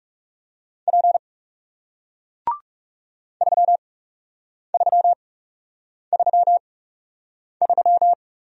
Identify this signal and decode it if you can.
Morse code, heard as a keyed tone, about 40 wpm, spaced as for 8 wpm, 700 Hz. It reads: P 3333